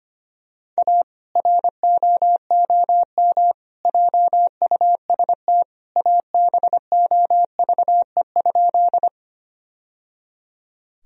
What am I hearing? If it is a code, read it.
Morse code, heard as a keyed tone, about 25 wpm, 700 Hz. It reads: A ROOM JUST ABOVE?
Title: Sentences from Top 1000 Words - Review Part 2 - 25wpm